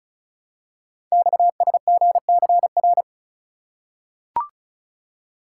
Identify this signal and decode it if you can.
Morse code, heard as a keyed tone, about 36 wpm, 700 Hz. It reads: XSGCR
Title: Code Groups - 5 Letters - 35wpm